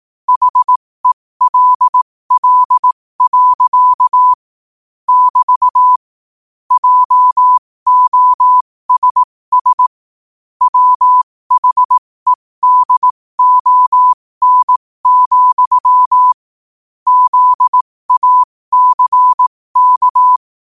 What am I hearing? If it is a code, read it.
Morse code, heard as a keyed tone, about 18 wpm, spaced as for 15 wpm, 1000 Hz. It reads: HELL. <BT> JOSS WHEDON, ZACK
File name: qotdfiles / 15wpm